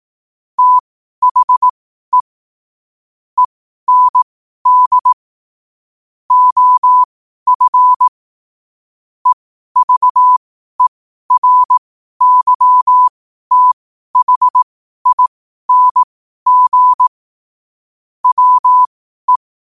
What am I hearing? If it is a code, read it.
Morse code, heard as a keyed tone, about 18 wpm, spaced as for 12 wpm, 1000 Hz. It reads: THE END OF EVERYTHING WE
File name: qotdfiles / 12wpm